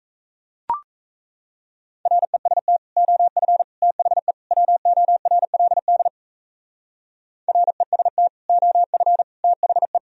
Morse code, heard as a keyed tone, about 35 wpm, 700 Hz. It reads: REST OF THE WORLD REST OF THE